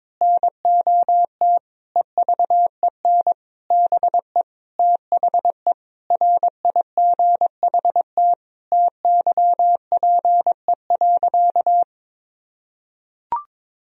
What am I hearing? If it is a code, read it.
Morse code, heard as a keyed tone, about 22 wpm, 700 Hz. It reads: NOT EVEN BE THE RIGHT TYPE.
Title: Sentences from Top 1000 Words - Review Part 3 - 22wpm